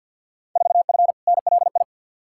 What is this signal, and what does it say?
Morse code, heard as a keyed tone, about 51 wpm, 700 Hz. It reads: 4F NLI
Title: Field Day Exchanges Encore - 50wpm